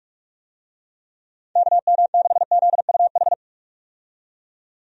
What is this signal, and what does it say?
Morse code, heard as a keyed tone, about 46 wpm, 700 Hz. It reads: KM6ZUH
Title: USA Call Signs - 6 Characters - 45wpm